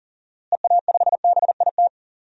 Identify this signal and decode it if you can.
Morse code, heard as a keyed tone, about 37 wpm, 700 Hz. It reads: EA5BIT
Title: DXCC-100 Call Signs - 6 Characters - Repeated 2x - 40wpm